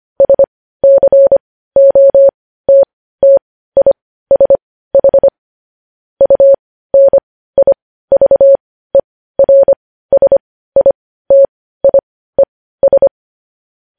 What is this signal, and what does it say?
Morse code, heard as a keyed tone, about 25 wpm, spaced as for 15 wpm, 550 Hz. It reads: SCOTTISH UNIVERSITIES
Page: news-headlines-in-morse-code-at-15-wpm